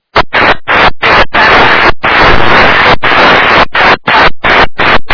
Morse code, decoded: none

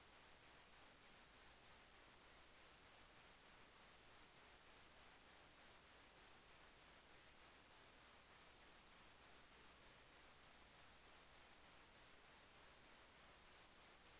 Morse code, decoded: none

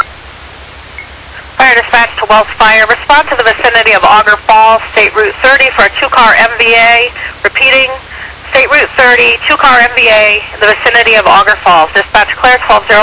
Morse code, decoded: none

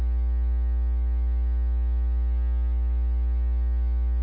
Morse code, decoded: none

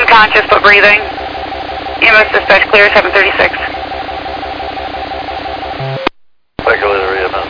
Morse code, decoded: none

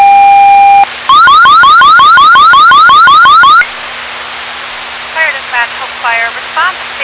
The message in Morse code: none

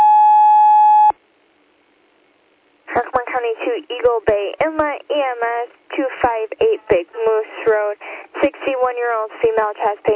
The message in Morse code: none

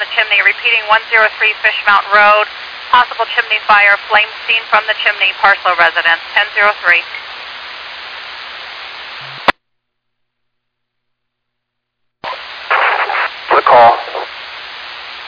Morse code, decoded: none

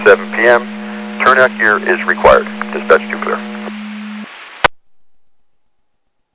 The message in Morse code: none

